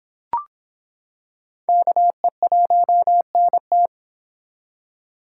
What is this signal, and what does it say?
Morse code, heard as a keyed tone, about 25 wpm, 700 Hz. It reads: KE1NT